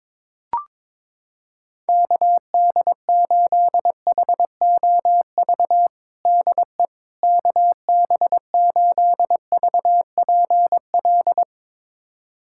Morse code, heard as a keyed tone, about 22 wpm, 700 Hz. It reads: KD8HOV DE KB8VPL
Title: USA Callsign DE Callsign - 6 Characters - 22wpm